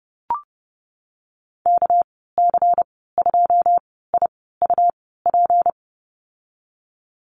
Morse code, heard as a keyed tone, about 31 wpm, spaced as for 17 wpm, 700 Hz. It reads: KC2IUP
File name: Lesson 37g - Letter x - Call Signs 30wpm